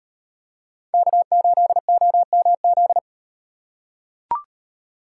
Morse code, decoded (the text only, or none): K8OMZ